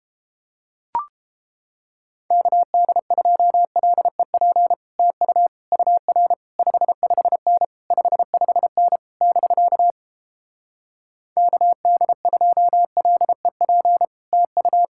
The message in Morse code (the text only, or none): KD2LEP TU UR 55N 55N <BK> KD2LEP TU